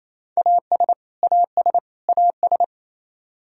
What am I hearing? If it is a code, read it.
Morse code, heard as a keyed tone, about 28 wpm, 700 Hz. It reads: AS AS AS
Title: Top 100 Words Repeated 28wpm